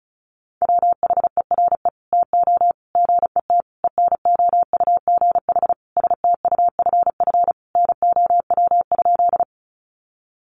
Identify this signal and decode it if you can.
Morse code, heard as a keyed tone, about 34 wpm, 700 Hz. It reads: WHERE TO GET ENOUGH STUFF NOW?
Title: Sentences from Top 1400 Words 35wpm